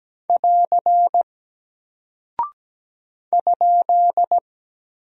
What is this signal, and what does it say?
Morse code, heard as a keyed tone, about 17 wpm, 700 Hz. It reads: <AR> ?
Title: Single Character (Letter:Number:Punctuation:Prosign) Alt 17wpm